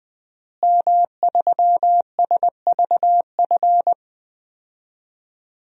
Morse code, heard as a keyed tone, about 20 wpm, 700 Hz. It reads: M3SVF
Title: DXCC-100 Call Signs - 5 Characters - Repeated 2x - 20wpm